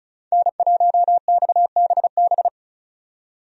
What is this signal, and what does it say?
Morse code, heard as a keyed tone, about 36 wpm, 700 Hz. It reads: N1XBB
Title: USA Call Signs - 5 Characters - 35wpm